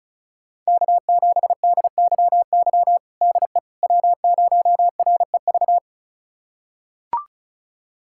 Morse code, heard as a keyed tone, about 34 wpm, 700 Hz. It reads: K7DYY DE W0REV
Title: USA Callsign DE Callsign - 5 Characters - 35wpm